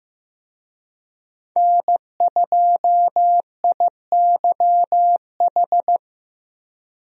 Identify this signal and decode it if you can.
Morse code, heard as a keyed tone, about 15 wpm, 700 Hz. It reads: N2IYH
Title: USA Call Signs - 5 Characters - 15wpm